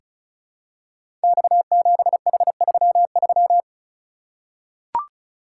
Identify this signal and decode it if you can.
Morse code, heard as a keyed tone, about 34 wpm, 700 Hz. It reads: X7H33
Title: Code Groups - 5 Letter:Numbers - 35wpm